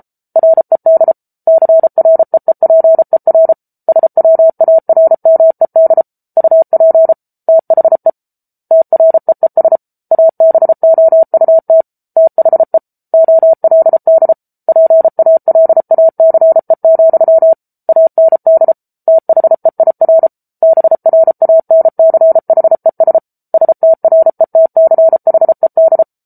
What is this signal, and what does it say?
Morse code, heard as a keyed tone, about 34 wpm, 650 Hz. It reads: RED CREEPER SWARMED UP THE TREES ABOUT THE OLD PALACE, AND THEIR BRANCHES STRETCHED
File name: Book Two / 07-Chapter 7 - 33 WPM